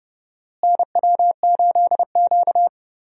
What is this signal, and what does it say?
Morse code, heard as a keyed tone, about 31 wpm, 700 Hz. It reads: NW8Q
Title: USA Call Signs - 4 Characters - Repeated 3x - 30wpm